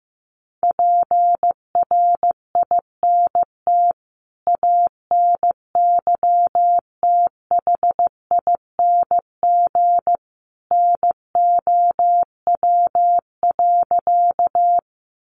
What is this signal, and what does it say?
Morse code, heard as a keyed tone, about 15 wpm, 700 Hz. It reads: PRINT ANYTHING NOW.
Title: Sentences from Top 1400 Words 15wpm